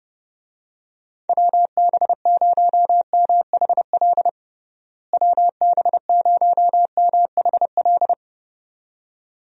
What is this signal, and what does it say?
Morse code, heard as a keyed tone, about 30 wpm, 700 Hz. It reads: WB0MHL WB0MHL
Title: USA Call Signs - 6 Characters - Repeated 2x - 30wpm